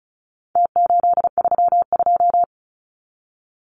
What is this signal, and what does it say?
Morse code, heard as a keyed tone, about 34 wpm, 700 Hz. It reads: T832
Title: Numbers - Contest Style - 4-Digit 35wpm